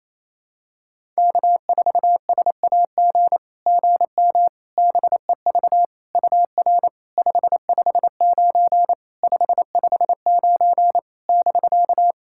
Morse code, heard as a keyed tone, about 28 wpm, 700 Hz. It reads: K4SAG GM BEV UR 559 559 <BK>
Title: SOTA QSO Practice 28wpm